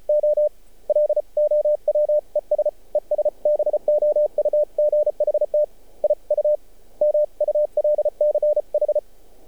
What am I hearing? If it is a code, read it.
Morse code, heard as a keyed tone, about 34 wpm, 600 Hz. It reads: O LOWES ES BOUGHT I U MULCH
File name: archived / CRSnet